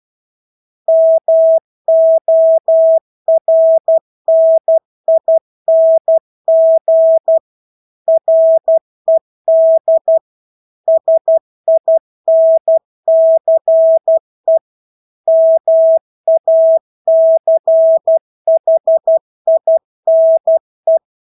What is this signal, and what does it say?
Morse code, heard as a keyed tone, about 12 wpm, 650 Hz. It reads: MORNING RED SINCE MACHINE
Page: Engelske ord 12 wpm